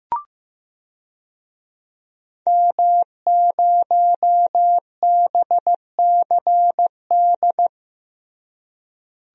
Morse code, heard as a keyed tone, about 15 wpm, 700 Hz. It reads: M0BCD
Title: DXCC-100 Call Signs - 5 Characters - Repeated 3x - 15wpm